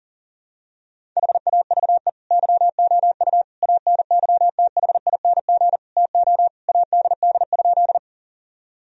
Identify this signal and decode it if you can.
Morse code, heard as a keyed tone, about 37 wpm, 700 Hz. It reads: HAVE YOU ANYTHING TO ADD?
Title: Sentences from Top 1000 Words - Review Part 1 - 40wpm